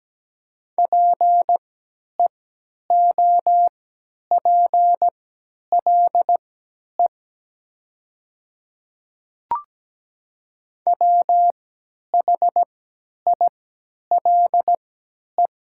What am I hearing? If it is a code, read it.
Morse code, heard as a keyed tone, about 17 wpm, spaced as for 10 wpm, 700 Hz. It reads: PEOPLE WHILE
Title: 3x Character Spacing - Top 300 Words - 17wpm